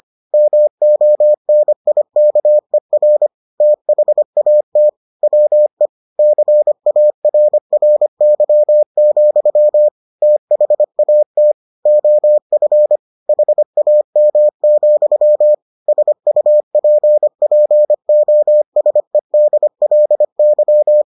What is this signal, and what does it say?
Morse code, heard as a keyed tone, about 25 wpm, 600 Hz. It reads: MONIKER THAT WE CARRY, THAT OF HAM, SUPPOSEDLY